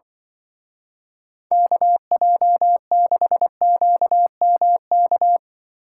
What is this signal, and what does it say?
Morse code, heard as a keyed tone, about 23 wpm, 700 Hz. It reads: KJ6QMK
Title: Speed Racing - USA 2x3 Call Signs 20wpm